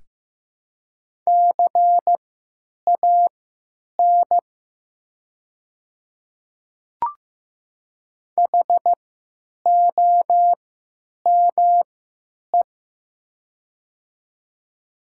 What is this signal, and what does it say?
Morse code, heard as a keyed tone, about 15 wpm, spaced as for 9 wpm, 700 Hz. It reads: CAN HOME
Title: 3x Character Spacing - Top 200 Words - 15wpm